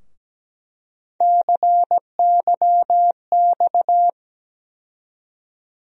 Morse code, heard as a keyed tone, about 17 wpm, 700 Hz. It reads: CYX
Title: USA Call Sign Postfixes - 3 Letters 17wpm